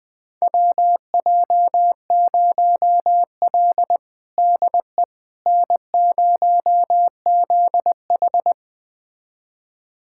Morse code, heard as a keyed tone, about 20 wpm, 700 Hz. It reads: WJ0L DE N0ZH
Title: USA Callsign DE Callsign - 4 Characters - 20wpm